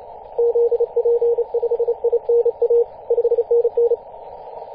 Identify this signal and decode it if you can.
Morse code, heard as a keyed tone, about 32 wpm, 470 Hz, with radio noise. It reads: ZP5INA 5NN
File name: posts / 2016-mar-18-2347z-5j0p-21-025mhz-cw